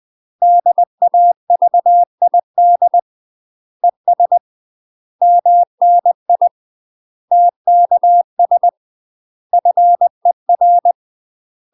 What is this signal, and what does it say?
Morse code, heard as a keyed tone, about 20 wpm, 700 Hz. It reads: DAVID ES MNI TKS FER
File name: UK MASTER QSOs TRAINING / 20WPM / UK MASTER QSOs 0047